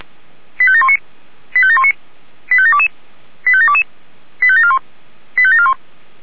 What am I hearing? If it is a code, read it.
Five-tone selective call (ZVEI), heard as quick runs of five tones, tones of about 70 ms. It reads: 85719 85719 85722 85722 85741 85741